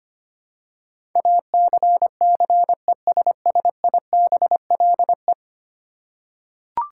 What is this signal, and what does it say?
Morse code, heard as a keyed tone, about 25 wpm, 700 Hz. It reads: ACCESSIBLE
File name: 10 Letter Words - First 3 Letters Given - 25wpm